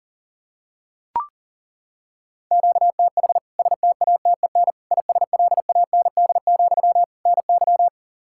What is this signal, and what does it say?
Morse code, heard as a keyed tone, about 40 wpm, 700 Hz. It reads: QTH STATEN ISLAND, NY
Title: K1USN SST QSO Practice 40wpm